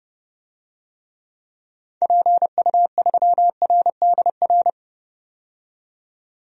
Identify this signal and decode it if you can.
Morse code, heard as a keyed tone, about 30 wpm, 700 Hz. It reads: PU3RDR
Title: DXCC-100 Call Signs - 6 Characters - 30wpm